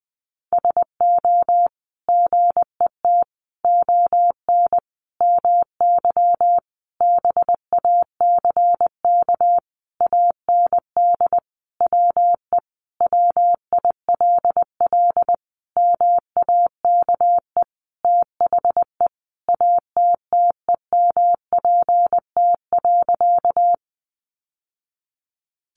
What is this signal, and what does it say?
Morse code, heard as a keyed tone, about 20 wpm, 700 Hz. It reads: SO GET ON MY BACK AND WE WILL MAKE THE ATTEMPT.
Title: Sentences from Top 1000 Words 20wpm